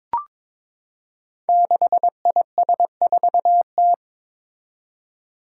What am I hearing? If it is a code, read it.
Morse code, heard as a keyed tone, about 22 wpm, 700 Hz. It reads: 6IS4T
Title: Five Challenging Letter-Number Combinations 22wpm